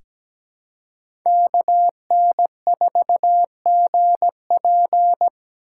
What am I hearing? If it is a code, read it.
Morse code, heard as a keyed tone, about 17 wpm, 700 Hz. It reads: KN4GP
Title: USA Call Signs - 5 Characters - 2x Character Spacing - 17wpm